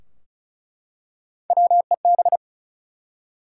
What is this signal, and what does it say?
Morse code, heard as a keyed tone, about 36 wpm, 700 Hz. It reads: WEB
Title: Three Letter Words - 35wpm